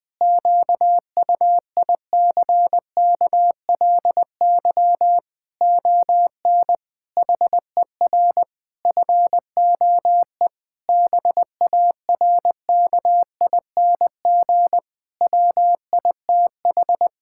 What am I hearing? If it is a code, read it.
Morse code, heard as a keyed tone, about 20 wpm, 700 Hz. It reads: QUICKLY ON HER FOE BARKING WITH
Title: Pangram Challenge v2 - 20wpm